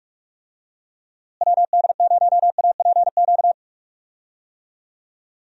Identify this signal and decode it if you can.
Morse code, heard as a keyed tone, about 46 wpm, 700 Hz. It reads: WD0APQ
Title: USA Call Signs - 6 Characters - Repeated 3x - 45wpm